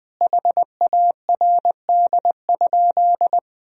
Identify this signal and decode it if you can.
Morse code, heard as a keyed tone, about 20 wpm, 700 Hz. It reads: HARD?
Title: Sentences from Top 1000 Words - Review Part 4 - 20wpm